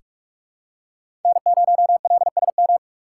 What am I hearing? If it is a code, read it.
Morse code, heard as a keyed tone, about 46 wpm, 700 Hz. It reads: N0LSM
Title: USA Call Signs - 5 Characters - 3x Character Spacing - 45wpm